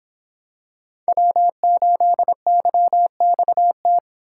Morse code, heard as a keyed tone, about 26 wpm, 700 Hz. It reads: W8YXT